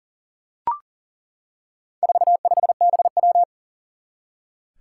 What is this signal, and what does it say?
Morse code, heard as a keyed tone, about 40 wpm, 700 Hz. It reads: 45BW
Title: Four Challenging Letter-Number Combinations 40wpm